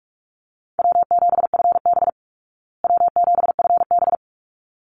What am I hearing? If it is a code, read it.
Morse code, heard as a keyed tone, about 44 wpm, 700 Hz. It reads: W7FB W7FB
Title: Call Signs Repeated 3x - 45wpm